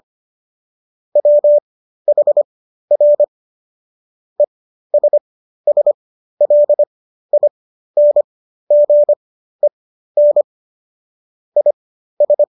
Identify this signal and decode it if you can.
Morse code, heard as a keyed tone, about 25 wpm, spaced as for 13 wpm, 590 Hz. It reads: WHR ESSLINGEN IS